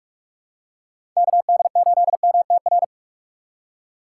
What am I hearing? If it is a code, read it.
Morse code, heard as a keyed tone, about 45 wpm, 700 Hz. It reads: KD8MTR
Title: USA Call Signs - 3x Character Spacing - 45wpm